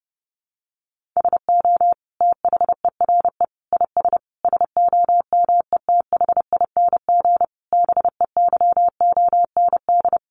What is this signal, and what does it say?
Morse code, heard as a keyed tone, about 30 wpm, 700 Hz. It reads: SO THERE IS SOMETHING BEYOND
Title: Sentences from Top 1500 Words 30wpm